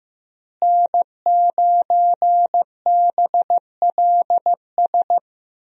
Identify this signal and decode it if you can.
Morse code, heard as a keyed tone, about 15 wpm, 700 Hz. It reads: N9BLS
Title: USA Call Signs - 5 Characters - Repeated 2x - 15wpm